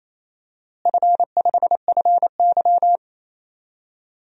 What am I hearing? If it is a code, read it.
Morse code, heard as a keyed tone, about 28 wpm, 700 Hz. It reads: F5FY
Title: DXCC-100 Call Signs - 4 Characters - Repeated 2x - 28wpm